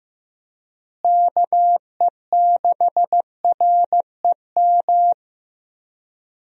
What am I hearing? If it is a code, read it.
Morse code, heard as a keyed tone, about 15 wpm, 700 Hz. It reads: KE6REM
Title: USA Call Signs - 6 Characters - Repeated 2x - 15wpm